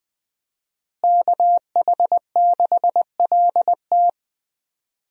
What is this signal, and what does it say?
Morse code, heard as a keyed tone, about 20 wpm, 700 Hz. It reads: KH6LT
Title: USA Call Signs - 5 Characters - Repeated 2x - 20wpm